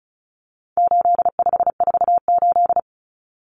Morse code, heard as a keyed tone, about 34 wpm, 700 Hz. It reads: 8548
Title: Numbers - 4-Digit 35wpm